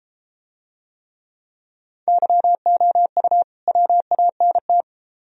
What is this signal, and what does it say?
Morse code, heard as a keyed tone, about 33 wpm, 700 Hz. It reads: YOU WANT